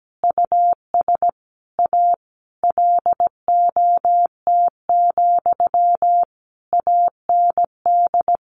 Morse code, heard as a keyed tone, about 17 wpm, 700 Hz. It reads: US A LOT, AND
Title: Sentences from Top 1300 Words 17wpm